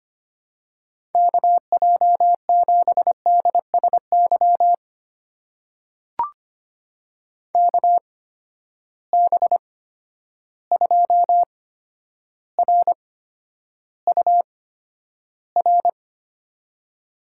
Morse code, heard as a keyed tone, about 25 wpm, spaced as for 11 wpm, 700 Hz. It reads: KJ7DSY KB2RUR
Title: USA Call Signs - 6 Characters - 8x Character Spacing - 25wpm